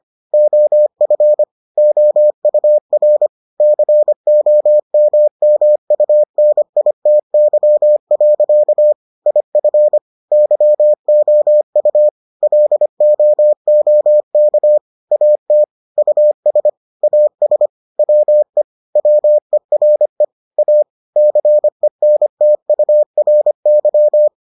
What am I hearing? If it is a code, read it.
Morse code, heard as a keyed tone, about 25 wpm, 600 Hz. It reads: OF OUR COMMUNITY. IF YOU LOOK AT US AS WE WERE A CENTURY